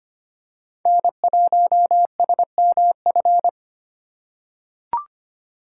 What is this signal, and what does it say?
Morse code, heard as a keyed tone, about 25 wpm, 700 Hz. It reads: N1SMF
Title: USA Call Signs - 5 Characters - 6x Character Spacing - 25wpm